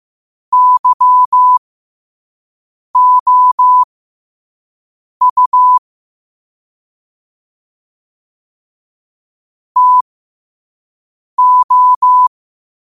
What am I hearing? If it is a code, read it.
Morse code, heard as a keyed tone, about 15 wpm, spaced as for 5 wpm, 1000 Hz. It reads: YOU TO
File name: qotdfiles / 5wpm